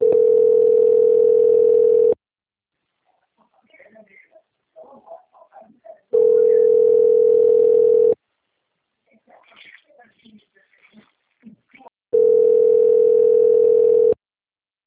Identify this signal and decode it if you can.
Ringback tone (North America), 440 + 480 Hz, with speech.
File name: out- - - - - .6566